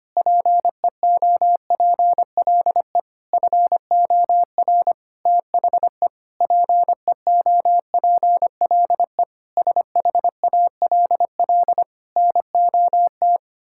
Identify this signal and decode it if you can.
Morse code, heard as a keyed tone, about 25 wpm, 700 Hz. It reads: PEOPLE FOR THE PEOPLE SHALL NOT
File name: practice / 25wpm / Sentences with 9-Word Phrases - 25wpm